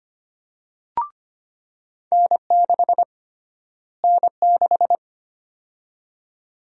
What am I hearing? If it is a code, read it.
Morse code, heard as a keyed tone, about 25 wpm, 700 Hz. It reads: N6 N6
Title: USA Call Sign Prefixes - Repeated 2x - 25wpm